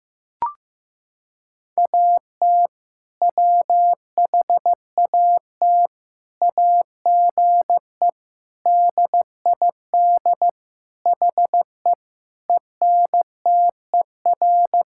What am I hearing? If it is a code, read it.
Morse code, heard as a keyed tone, about 15 wpm, 690 Hz. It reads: AT WHAT AGE DID HE ENTER
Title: Sentences from Top 1000 Words - Review Part 5 - 15wpm